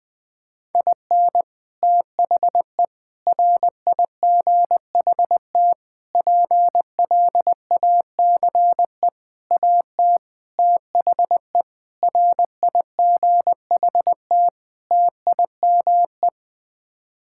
Morse code, heard as a keyed tone, about 20 wpm, 700 Hz. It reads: IN THE RIGHT PLACE AT THE RIGHT TIME
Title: Sets of 8 Words - 20wpm